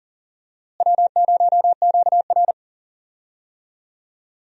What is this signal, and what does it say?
Morse code, heard as a keyed tone, about 37 wpm, 700 Hz. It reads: W0QR
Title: USA Call Signs - 4 Characters - Repeated 2x - 40wpm